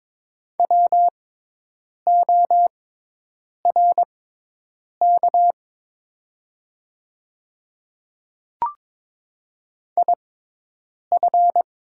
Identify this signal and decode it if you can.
Morse code, heard as a keyed tone, about 22 wpm, spaced as for 8 wpm, 700 Hz. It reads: WORK IF
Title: 6x Character Spacing - Top 500 Words - 22wpm